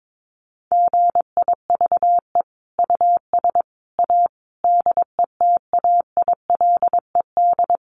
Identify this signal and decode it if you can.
Morse code, heard as a keyed tone, about 22 wpm, 700 Hz. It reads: GIVE US A DETAILED
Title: Sentences from Top 1500 Words 22wpm